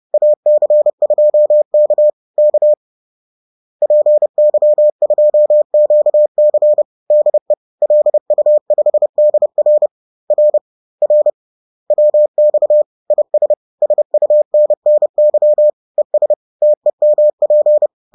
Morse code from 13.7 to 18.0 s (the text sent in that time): SUNNY ES TEMP